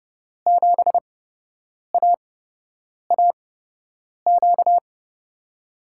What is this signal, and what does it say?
Morse code, heard as a keyed tone, about 29 wpm, spaced as for 8 wpm, 700 Hz. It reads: 7AAQ
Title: USA Call Signs - 6 Characters - 8x Character Spacing - 30wpm